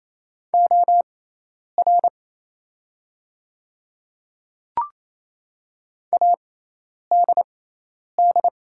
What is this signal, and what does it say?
Morse code, heard as a keyed tone, about 27 wpm, spaced as for 10 wpm, 700 Hz. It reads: OR ADD
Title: 6x Character Spacing - Top 200 Words - 28wpm